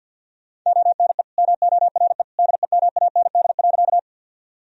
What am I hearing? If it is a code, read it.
Morse code, heard as a keyed tone, about 50 wpm, 700 Hz. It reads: ONE MORE DEMAND.
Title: Sentences from Top 1000 Words - Review Part 4 - 50wpm